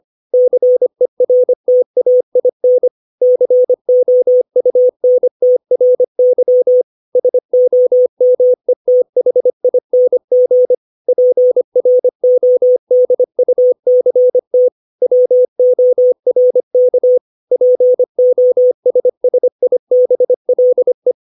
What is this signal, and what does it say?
Morse code, heard as a keyed tone, about 25 wpm, 500 Hz. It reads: CERTAIN COUNTRY SOMETHING PRODUCT WORK POSSIBLE
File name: 632 words / 632 Words 25WPM